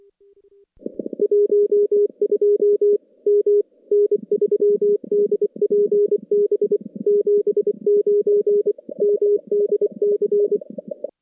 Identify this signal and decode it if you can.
Morse code, heard as a keyed tone, about 24 wpm, 410 Hz. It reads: 12 M N3DPB 79 MDC
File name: contests / CW-SS-k5tr- -2200